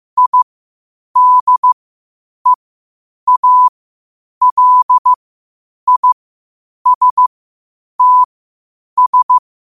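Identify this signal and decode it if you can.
Morse code, heard as a keyed tone, about 15 wpm, spaced as for 8 wpm, 1000 Hz. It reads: IDEALISTS
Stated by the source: Morse code at 8 wpm.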